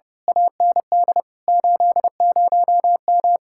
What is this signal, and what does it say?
Morse code, heard as a keyed tone, about 33 wpm, 700 Hz. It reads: AND 80M
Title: Speed-Racing Common QSO Phrases - 30wpm